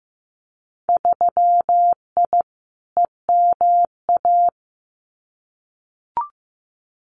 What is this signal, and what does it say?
Morse code, heard as a keyed tone, about 15 wpm, 700 Hz. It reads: 3I EMA
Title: Winter Field Day Exchanges Encore 15wpm